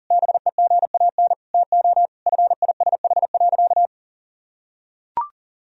Morse code, heard as a keyed tone, about 40 wpm, 700 Hz. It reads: BEGAN TO FISH.
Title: Sentences from Top 1000 Words - Review Part 1 - 40wpm